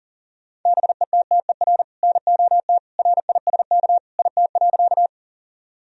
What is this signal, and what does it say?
Morse code, heard as a keyed tone, about 40 wpm, 700 Hz. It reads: BETTER NOT RISK IT.
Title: Sentences from Top 1000 Words - Review Part 4 - 40wpm